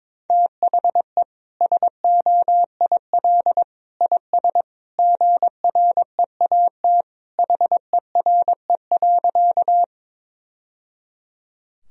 Morse code, heard as a keyed tone, about 21 wpm, 700 Hz. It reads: THE SOIL IS GREAT HERE.
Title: Sentences from Top 1000 Words - Review Part 4 - 22wpm